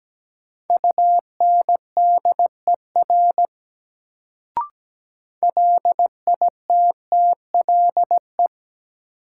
Morse code, heard as a keyed tone, about 17 wpm, 700 Hz. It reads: UNDER LITTLE
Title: Sets of 2 Words - New Word Drill - 17wpm